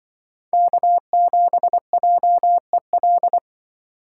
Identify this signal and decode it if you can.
Morse code, heard as a keyed tone, about 24 wpm, 700 Hz. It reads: K7JEL
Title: Speed Racing - USA Call Signs 20wpm